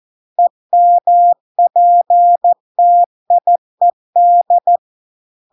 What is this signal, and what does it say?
Morse code, heard as a keyed tone, about 14 wpm, 700 Hz. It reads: EMPTIED